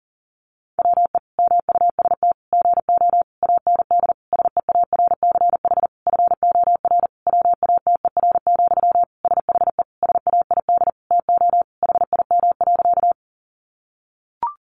Morse code, heard as a keyed tone, about 40 wpm, 700 Hz. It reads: WE MUST GO AND SEARCH FOR WATER, SHE SAID TO HIM.
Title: Sentences from Top 800 Words 40wpm